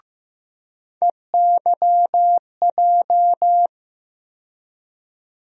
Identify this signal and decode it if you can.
Morse code, heard as a keyed tone, about 15 wpm, 700 Hz. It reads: EYJ
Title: USA Call Sign Postfixes - 3 Letters 15wpm